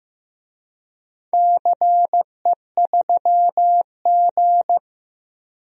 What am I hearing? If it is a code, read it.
Morse code, heard as a keyed tone, about 15 wpm, 700 Hz. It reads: CE3G
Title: DXCC-100 Call Signs - 4 Characters - Repeated 2x - 15wpm